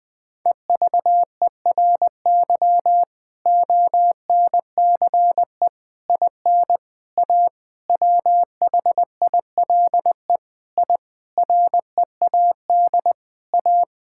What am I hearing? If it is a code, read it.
Morse code, heard as a keyed tone, about 20 wpm, 700 Hz. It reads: EVERY ONCE IN A WHILE I READ A
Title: Sentences with 5-Word Phrases - 20wpm